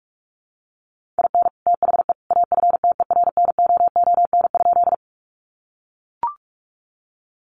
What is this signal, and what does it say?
Morse code, heard as a keyed tone, about 44 wpm, 700 Hz. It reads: IN THE AFTERNOON?
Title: Sentences from Top 1500 Words 45wpm